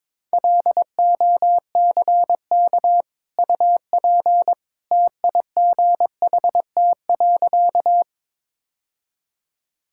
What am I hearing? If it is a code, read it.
Morse code, heard as a keyed tone, about 21 wpm, 700 Hz. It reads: LOCK UP TIGHT.
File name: practice / 22wpm / Easy Sentences with Hints - 22wpm